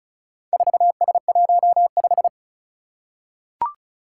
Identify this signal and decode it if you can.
Morse code, heard as a keyed tone, about 36 wpm, 700 Hz. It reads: 4S15
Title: Four Challenging Letter-Number Combinations 35wpm